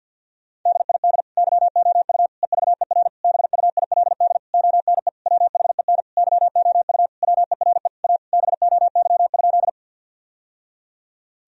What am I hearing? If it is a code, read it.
Morse code, heard as a keyed tone, about 45 wpm, 700 Hz. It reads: DID YOU EVER BUILD ONE WHEN YOU WERE A BOY?
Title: Sentences from Top 1000 Words - Review Part 1 - 50wpm